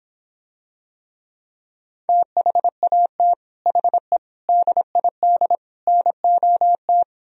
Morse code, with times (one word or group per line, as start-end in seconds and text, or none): 2.1-3.3 THAT
3.7-4.2 HE
4.5-5.6 DID
5.9-7.0 NOT